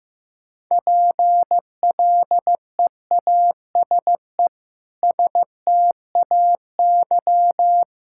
Morse code, heard as a keyed tone, about 15 wpm, 700 Hz. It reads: PLEASE STAY
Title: Sentences from Top 1600 Words 15wpm